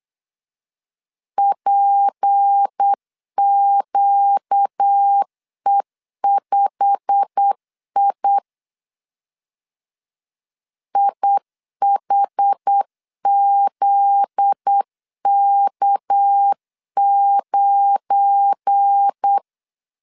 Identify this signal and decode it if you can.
Morse code, heard as a keyed tone, about 8 wpm, 800 Hz. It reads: PQE5I IHZK9
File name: CW / DET / 1 9 1500C S B 8wpm 5624